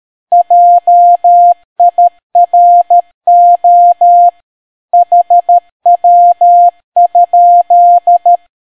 Morse code, heard as a keyed tone, about 13 wpm, 700 Hz. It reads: JIRO HW?